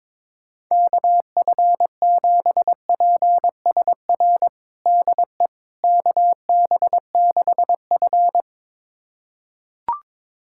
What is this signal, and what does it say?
Morse code, heard as a keyed tone, about 22 wpm, 700 Hz. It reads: KF7PSR DE KB6F
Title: USA Callsign DE Callsign 22wpm